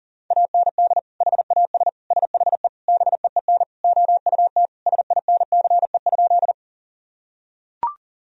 Morse code, heard as a keyed tone, about 40 wpm, 700 Hz. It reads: AND HAS SHE BEEN OUT SINCE?
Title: Sentences from Top 1000 Words - Review Part 1 - 40wpm